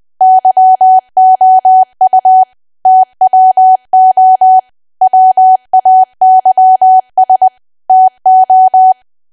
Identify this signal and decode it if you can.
Morse code, heard as a keyed tone, about 20 wpm, 750 Hz. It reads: YOU TWO WAYS TO
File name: cwa / W1AW-Level3 / 20WPM